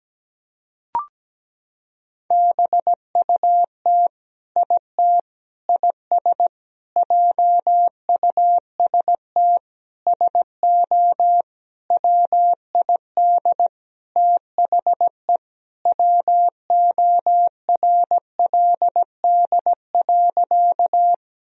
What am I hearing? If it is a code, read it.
Morse code, heard as a keyed tone, about 17 wpm, 700 Hz. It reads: BUT IT IS JUST SO WID THE WORLD.
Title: Sentences from Top 200 Words and Abbreviations 17wpm